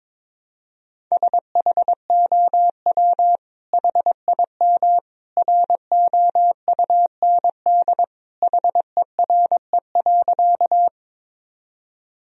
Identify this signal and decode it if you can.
Morse code, heard as a keyed tone, about 22 wpm, 700 Hz. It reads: SHOW HIM ROUND HERE.
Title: Sentences from Top 1000 Words - Review Part 1 - 22wpm